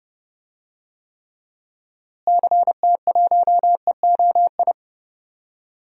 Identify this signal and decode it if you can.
Morse code, heard as a keyed tone, about 32 wpm, 700 Hz. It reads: CT1EOI
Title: DXCC-100 Call Signs - 6 Characters - Repeated 3x - 30wpm